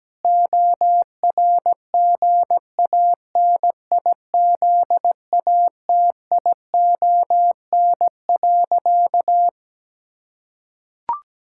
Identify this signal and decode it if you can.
Morse code, heard as a keyed tone, about 17 wpm, 700 Hz. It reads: ORGANIZATION.
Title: Sentences from Top 1000 Words - Review Part 5 - 17wpm